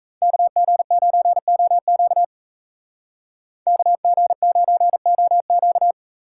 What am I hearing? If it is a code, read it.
Morse code, heard as a keyed tone, about 33 wpm, 700 Hz. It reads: KG9OQ KG9OQ
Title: Speed Racing - USA 2x2 Call Signs 28wpm